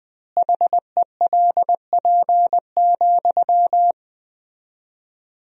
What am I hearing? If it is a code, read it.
Morse code, heard as a keyed tone, about 20 wpm, 700 Hz. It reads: HELP,
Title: 4x Word-Spacing - Sentences from Top 300 Words - 20wpm